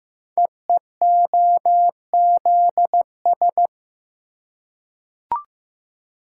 Morse code, heard as a keyed tone, about 15 wpm, 700 Hz. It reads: EEOZS
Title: Code Groups - 5 Letters - 15wpm